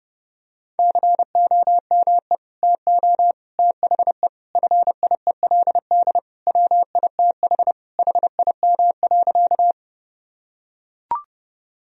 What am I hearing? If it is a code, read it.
Morse code, heard as a keyed tone, about 32 wpm, 700 Hz. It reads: COME TO THE FIELD WITH HIM.
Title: Sentences from Top 1000 Words - Review Part 2 - 30wpm